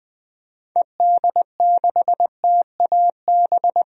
Morse code, heard as a keyed tone, about 20 wpm, 700 Hz. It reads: ED6TAB
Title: DXCC-100 Call Signs - 6 Characters - Repeated 3x - 20wpm